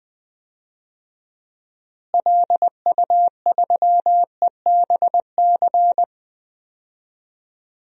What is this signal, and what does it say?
Morse code, heard as a keyed tone, about 21 wpm, 700 Hz. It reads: LU3EBC
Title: DXCC-100 Call Signs - 6 Characters - Repeated 3x - 20wpm